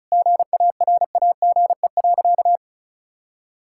Morse code, heard as a keyed tone, about 36 wpm, 700 Hz. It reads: GARAGE.
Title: Sentences from Top 1600 Words 35wpm